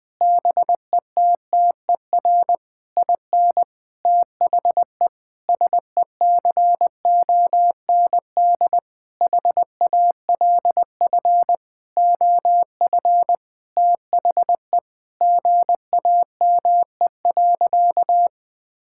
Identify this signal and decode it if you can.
Morse code, heard as a keyed tone, about 20 wpm, 700 Hz. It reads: BETTER IN THE SECOND HALF OF THE GAME.
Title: Sentences with 5-Word Phrases - 20wpm